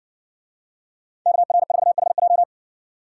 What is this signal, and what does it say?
Morse code, heard as a keyed tone, about 60 wpm, 700 Hz. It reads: KR4HJ